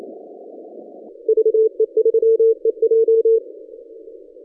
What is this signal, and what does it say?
Morse code, heard as a keyed tone, about 29 wpm, 450 Hz. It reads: VE3EJ
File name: 2015 cq ww cw / 2015 cqwww k5tr - 0-2